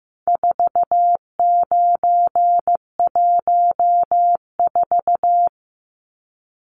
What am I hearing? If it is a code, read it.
Morse code, heard as a keyed tone, about 15 wpm, 700 Hz. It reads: 4914
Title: Numbers - 4-Digit 15wpm